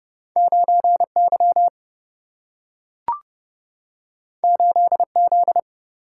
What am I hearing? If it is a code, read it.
Morse code, heard as a keyed tone, about 30 wpm, 700 Hz. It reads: 9 Y 8 Z
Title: Single-Letter Single-Number 30wpm